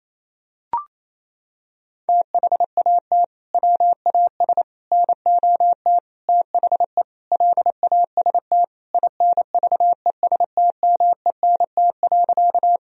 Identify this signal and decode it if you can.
Morse code, heard as a keyed tone, about 28 wpm, 700 Hz. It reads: THAT WAS NOT THE LAST INVESTMENT.
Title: Sentences from Top 1000 Words - Review Part 4 - 28wpm